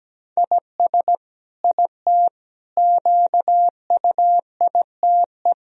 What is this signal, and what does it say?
Morse code, heard as a keyed tone, about 17 wpm, 700 Hz. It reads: IS IT QUITE
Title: Sentences from Top 1000 Words - Review Part 3 - 17wpm